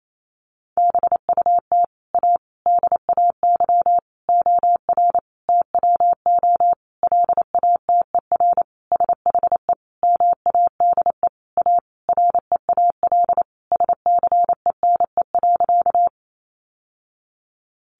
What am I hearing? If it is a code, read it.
Morse code, heard as a keyed tone, about 28 wpm, 700 Hz. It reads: BUT A DAY OR TWO LATER SHE MADE A REAL SCENE.
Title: Sentences from Top 1100 Words 28wpm